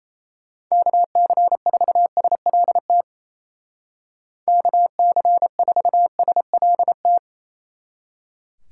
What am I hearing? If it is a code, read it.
Morse code, heard as a keyed tone, about 35 wpm, 700 Hz. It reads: KC4SLT KC4SLT
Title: Speed Racing - USA 2x3 Call Signs 28wpm